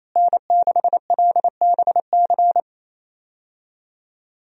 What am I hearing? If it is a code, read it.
Morse code, heard as a keyed tone, about 28 wpm, 700 Hz. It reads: N6LBC
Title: USA Call Signs - 5 Characters - Repeated 3x - 28wpm